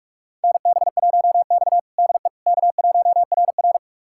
Morse code, heard as a keyed tone, about 42 wpm, 700 Hz. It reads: NB1X DE K1RR
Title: USA Callsign DE Callsign - 4 Characters - 45wpm